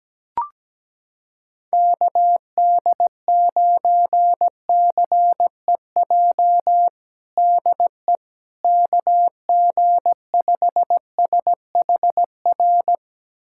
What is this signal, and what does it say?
Morse code, heard as a keyed tone, about 17 wpm, 700 Hz. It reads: KD9CEJ DE KG5SHR
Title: USA Callsign DE Callsign - 6 Characters - 17wpm